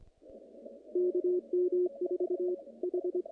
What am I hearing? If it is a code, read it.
Morse code, heard as a keyed tone, about 25 wpm, 360 Hz, with radio noise. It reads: KM4H
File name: contests / naqp-cw-k5tr-op-k5pi- -2000